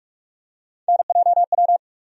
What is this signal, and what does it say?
Morse code, heard as a keyed tone, about 46 wpm, 700 Hz. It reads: NJW
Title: USA Call Sign Postfixes - 3 Letters 45wpm